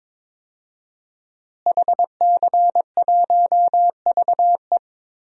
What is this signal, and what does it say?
Morse code, heard as a keyed tone, about 22 wpm, 700 Hz. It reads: HC1VE
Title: DXCC-100 Call Signs - 5 Characters - Repeated 2x - 22wpm